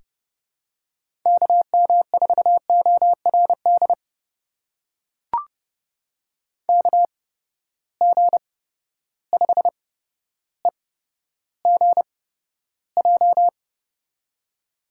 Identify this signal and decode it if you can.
Morse code, heard as a keyed tone, about 30 wpm, spaced as for 13 wpm, 700 Hz. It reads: KM4ORD KG5EGJ
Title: USA Call Signs - 8x Character Spacing - 30wpm